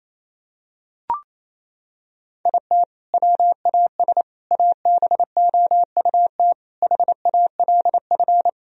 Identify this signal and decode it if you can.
Morse code, heard as a keyed tone, about 28 wpm, 700 Hz. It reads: IT WAS ABOUT HALF
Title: Sentences from Top 1000 Words - Review Part 3 - 28wpm